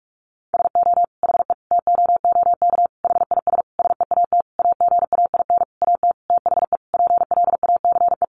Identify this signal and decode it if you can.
Morse code, heard as a keyed tone, about 46 wpm, 700 Hz. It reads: SO HE TOOK HIS SEAT AGAIN AT THE PLACE
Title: Sentences from Top 1500 Words 45wpm